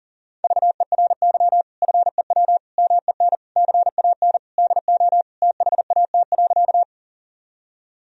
Morse code, heard as a keyed tone, about 40 wpm, 700 Hz. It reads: VERY FEW MEN CAN DO THAT.
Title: Sentences from Top 1000 Words - Review Part 1 - 40wpm